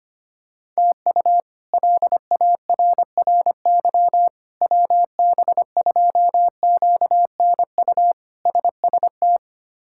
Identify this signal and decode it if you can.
Morse code, heard as a keyed tone, about 25 wpm, 700 Hz. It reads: TU LARRY WB2QNU SST
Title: K1USN SST QSO Practice - 25wpm